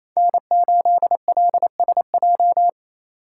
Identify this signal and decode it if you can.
Morse code, heard as a keyed tone, about 33 wpm, 700 Hz. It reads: N8LSJ
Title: Speed Racing - USA Call Signs 28wpm